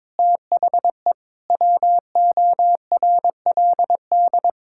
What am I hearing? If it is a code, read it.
Morse code, heard as a keyed tone, about 22 wpm, 700 Hz. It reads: THE WORLD
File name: practice / 22wpm / Sets of 6 Words - 22wpm